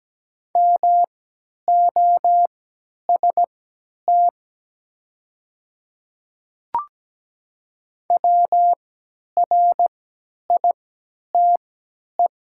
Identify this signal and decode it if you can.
Morse code, heard as a keyed tone, about 17 wpm, spaced as for 10 wpm, 700 Hz. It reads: MOST WRITE
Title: 3x Character Spacing - Top 100 Words - 17wpm